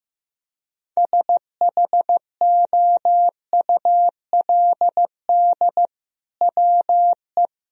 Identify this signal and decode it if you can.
Morse code, heard as a keyed tone, about 16 wpm, 700 Hz. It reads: SHOULD WE
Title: Sentences from Top 1000 Words - Review Part 3 - 15wpm